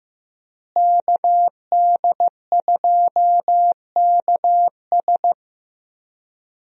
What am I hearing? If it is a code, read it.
Morse code, heard as a keyed tone, about 15 wpm, 700 Hz. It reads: KD2KS
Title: USA Call Signs - 5 Characters - Repeated 3x - 15wpm